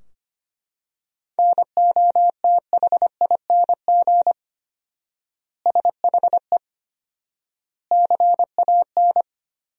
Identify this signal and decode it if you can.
Morse code, heard as a keyed tone, about 25 wpm, 700 Hz. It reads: NOTHING SHE CAN
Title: 4x Word-Spacing - Sentences from Top 500 Words - 25wpm